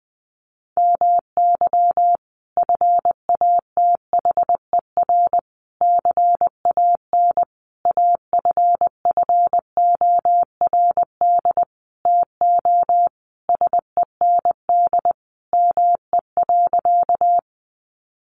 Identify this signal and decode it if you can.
Morse code, heard as a keyed tone, about 20 wpm, 700 Hz. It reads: MY FATHER CAN AFFORD TO SEND ME.
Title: Sentences from Top 1500 Words 20wpm